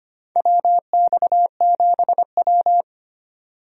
Morse code, heard as a keyed tone, about 25 wpm, 700 Hz. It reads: WX7W
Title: USA Call Signs - 4 Characters - 25wpm